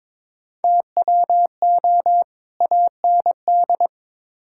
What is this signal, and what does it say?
Morse code, heard as a keyed tone, about 22 wpm, 700 Hz. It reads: TWO AND
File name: practice / 22wpm / Build A Phrase - 5-Word Phrases - 22wpm